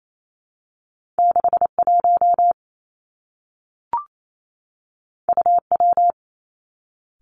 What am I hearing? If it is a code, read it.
Morse code, heard as a keyed tone, about 28 wpm, 700 Hz. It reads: 6 1 U W
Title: Two Challenging Letter-Number Combinations 28wpm